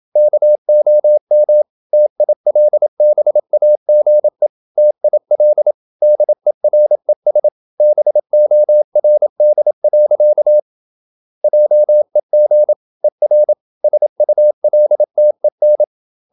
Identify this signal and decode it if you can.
Morse code, heard as a keyed tone, about 27 wpm, 600 Hz. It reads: KOM TILBAGE TIL DERES BORD. JEG ER SULTEN
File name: Ferie QRQ 27wpm